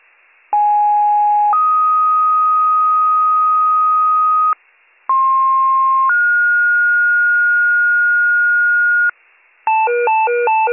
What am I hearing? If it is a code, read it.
Two-tone page, heard as a one-second tone, then a three-second tone, about 830 then 1250 Hz, about 1060 then 1430 Hz, with noise. It is heard from 0.5 to 9.1 s.